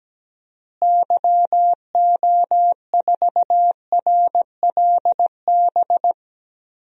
Morse code, heard as a keyed tone, about 17 wpm, 700 Hz. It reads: YO4RLB
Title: DXCC-100 Call Signs - 6 Characters - Repeated 2x - 17wpm